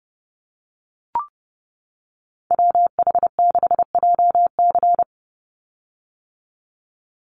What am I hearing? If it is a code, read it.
Morse code, heard as a keyed tone, about 30 wpm, 700 Hz. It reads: WH6JC
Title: USA Call Signs 30wpm